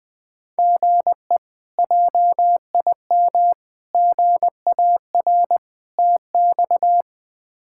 Morse code, heard as a keyed tone, about 21 wpm, 700 Hz. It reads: GE JIM GAR TX
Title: K1USN SST QSO Practice - 20wpm